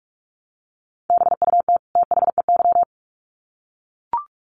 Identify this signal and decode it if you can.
Morse code, heard as a keyed tone, about 44 wpm, 700 Hz. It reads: BUT THEY
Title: Sets of 2 Words 45wpm